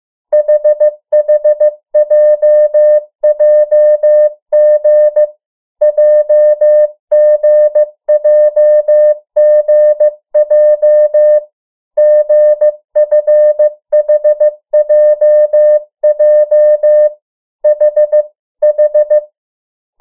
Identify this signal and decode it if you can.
Morse code, heard as a keyed tone, about 15 wpm, 600 Hz. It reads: HHJJGJGJGJGFHJJHH
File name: morse mp3 / 03 urok 2